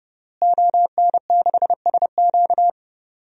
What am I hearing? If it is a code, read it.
Morse code, heard as a keyed tone, about 32 wpm, 700 Hz. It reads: ON6SQ
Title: DXCC-100 Call Signs - 5 Characters - Repeated 3x - 30wpm